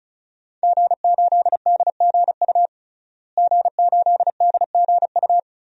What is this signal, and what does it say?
Morse code, heard as a keyed tone, about 34 wpm, 700 Hz. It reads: G8DGU G8DGU
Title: DXCC-100 Call Signs - 5 Characters - Repeated 3x - 35wpm